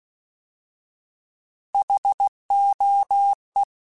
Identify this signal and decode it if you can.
Morse code, heard as a keyed tone, about 16 wpm, 800 Hz. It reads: HOE
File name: Alphabetical / Alphabetical mp3s / Hs @15